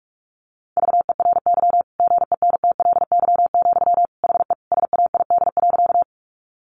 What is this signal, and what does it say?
Morse code, heard as a keyed tone, about 44 wpm, 700 Hz. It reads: VERY GENTLY, HE SAID.
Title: Sentences from Top 1500 Words 45wpm